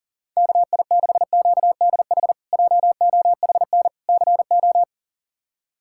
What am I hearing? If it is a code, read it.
Morse code, heard as a keyed tone, about 37 wpm, 700 Hz. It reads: KI6QDH JOHN CO
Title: K1USN SST QSO Practice - 40wpm